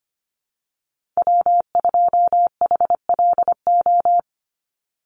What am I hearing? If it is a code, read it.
Morse code, heard as a keyed tone, about 25 wpm, 700 Hz. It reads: W2HLO
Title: USA Call Signs 25wpm